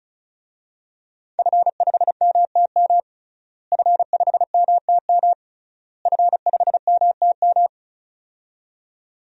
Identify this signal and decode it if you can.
Morse code, heard as a keyed tone, about 34 wpm, 700 Hz. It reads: F5MTM F5MTM F5MTM
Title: DXCC-100 Call Signs - 5 Characters - Repeated 3x - 35wpm